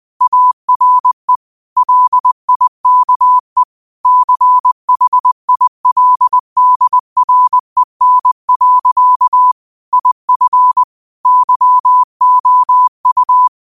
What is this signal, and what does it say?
Morse code, heard as a keyed tone, about 20 wpm, 1000 Hz. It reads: ARE LIKE CHILDREN. IF YOU
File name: qotdfiles / 20wpm